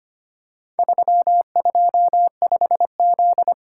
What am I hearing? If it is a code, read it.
Morse code, heard as a keyed tone, about 25 wpm, 700 Hz. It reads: 325Z
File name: Four Letter-Number Combinations - 25wpm